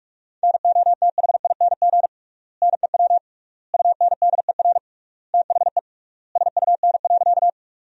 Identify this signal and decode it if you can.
Morse code, heard as a keyed tone, about 46 wpm, 700 Hz. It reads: NOTHING NEW UNDER THE SUN.
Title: 3x Word-Spacing - Sentences from Top 500 Words - 45wpm